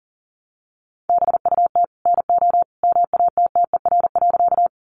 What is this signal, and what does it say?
Morse code, heard as a keyed tone, about 37 wpm, 700 Hz. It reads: BUT NO MATTER.
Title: Sentences from Top 700 Words 40wpm